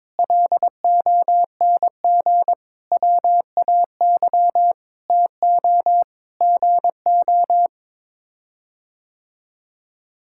Morse code, heard as a keyed tone, about 22 wpm, 700 Hz. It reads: LONG WAY TO GO
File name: practice / 22wpm / Sets of 7 Words - 22wpm